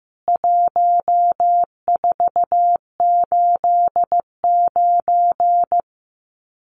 Morse code, heard as a keyed tone, about 15 wpm, 700 Hz. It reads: 1489
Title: CWT Exchange 15wpm